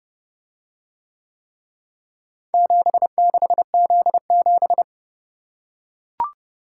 Morse code, heard as a keyed tone, about 29 wpm, 700 Hz. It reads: 76Z7